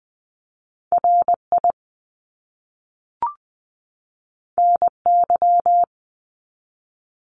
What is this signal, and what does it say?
Morse code, heard as a keyed tone, about 20 wpm, 700 Hz. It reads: RI NY
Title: US States 20wpm